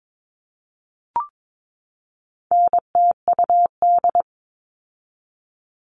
Morse code, heard as a keyed tone, about 22 wpm, 700 Hz. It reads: NTUD
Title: All Cut Numbers - 4-Digit 22wpm